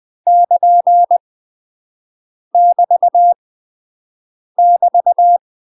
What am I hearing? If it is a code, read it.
Morse code, heard as a keyed tone, about 20 wpm, 700 Hz. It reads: <KN> <BT> <BT>
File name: UK MASTER QSOs TRAINING / 20WPM / UK MASTER QSOs 0043